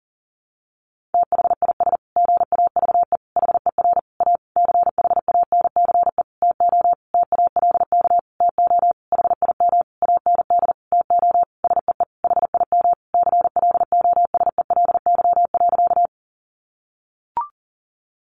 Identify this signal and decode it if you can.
Morse code, heard as a keyed tone, about 40 wpm, 700 Hz. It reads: THIS GAVE HER A CHANCE TO TALK TO HIM AND TO SEE HIM CLOSELY.
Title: Sentences from Top 1400 Words 40wpm